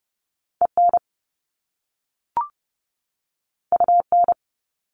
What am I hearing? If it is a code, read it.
Morse code, heard as a keyed tone, about 30 wpm, 700 Hz. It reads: EN UN
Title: Top Two Letter Combinations in Words 30wpm